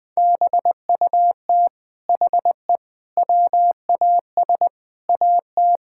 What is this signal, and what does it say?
Morse code, heard as a keyed tone, about 21 wpm, 700 Hz. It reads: BUT HE WAS AT
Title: Sentences from Top 1000 Words - Review Part 5 - 20wpm